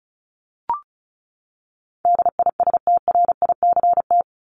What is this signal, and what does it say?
Morse code, heard as a keyed tone, about 34 wpm, 700 Hz. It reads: DISTRICT
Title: Top 1501-1750 Words 35wpm